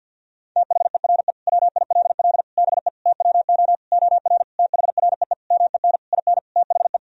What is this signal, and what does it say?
Morse code, heard as a keyed tone, about 50 wpm, 700 Hz. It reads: THERE WILL BE TWO OR THREE MEN IN THE
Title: Sentences from Top 1000 Words - Review Part 3 - 50wpm